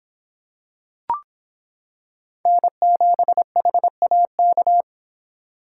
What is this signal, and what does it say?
Morse code, heard as a keyed tone, about 26 wpm, 700 Hz. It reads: N7HAK